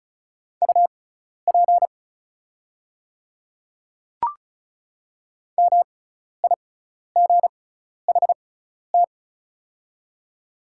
Morse code, heard as a keyed tone, about 36 wpm, spaced as for 12 wpm, 700 Hz. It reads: UP MIGHT
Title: 6x Character Spacing - Top 300 Words - 35wpm